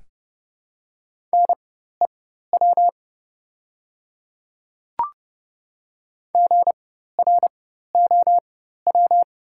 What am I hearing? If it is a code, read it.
Morse code, heard as a keyed tone, about 30 wpm, spaced as for 14 wpm, 700 Hz. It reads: NEW GROW
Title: 4x Character Spacing - Top 300 Words - 30wpm